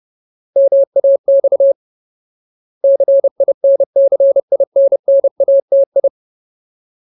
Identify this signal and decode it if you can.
Morse code, heard as a keyed tone, about 30 wpm, 550 Hz. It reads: MAX CINCINNATI
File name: Ultimate Word List / 30WPM Ultimate